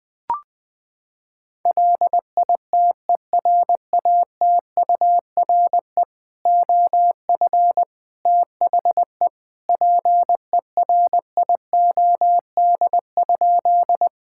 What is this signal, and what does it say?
Morse code, heard as a keyed tone, about 20 wpm, 700 Hz. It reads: LITERATURE OF THE PERIOD?
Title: Sentences from Top 1000 Words - Review Part 5 - 20wpm